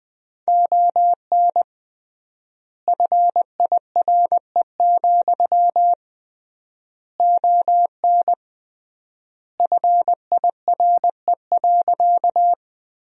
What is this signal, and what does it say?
Morse code, heard as a keyed tone, about 21 wpm, 700 Hz. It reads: ON FIRE, ON FIRE.
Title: 3x Word-Spacing - Sentences from Top 400 Words - 20wpm